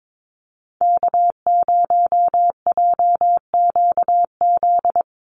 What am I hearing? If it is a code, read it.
Morse code, heard as a keyed tone, about 22 wpm, 700 Hz. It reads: K0JQZ
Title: NA SOTA Activator Callsigns 22wpm